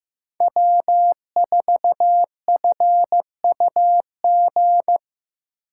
Morse code, heard as a keyed tone, about 15 wpm, 700 Hz. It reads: W4FUG
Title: USA Call Signs - 5 Characters - 3x Character Spacing - 15wpm